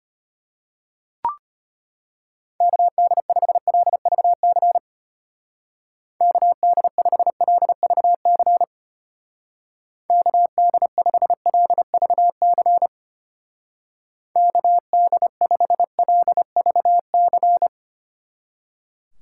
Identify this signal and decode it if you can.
Morse code, heard as a keyed tone, about 31 wpm, 700 Hz. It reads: KD5LVC KD5LVC KD5LVC KD5LVC